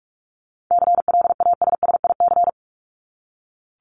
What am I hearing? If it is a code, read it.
Morse code, heard as a keyed tone, about 44 wpm, 700 Hz. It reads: CLASSIC
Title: Top 751-1000 Words 45wpm